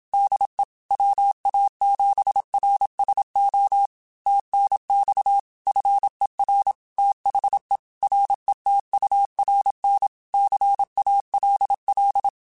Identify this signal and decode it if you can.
Morse code, heard as a keyed tone, about 27 wpm, 800 Hz. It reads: DE WA7RSO TNX FER THE RETURN CALL
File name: QSOs / QSOs-mp3s / kc7sac@25